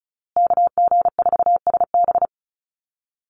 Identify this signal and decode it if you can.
Morse code, heard as a keyed tone, about 34 wpm, 700 Hz. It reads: KG4SB
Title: USA Call Signs Repeated 3x 35wpm